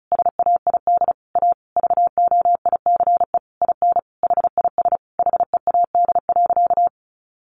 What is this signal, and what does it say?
Morse code, heard as a keyed tone, about 34 wpm, 700 Hz. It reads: SAID A VOICE IN HIS HEAD.
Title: Sentences from Top 1300 Words 35wpm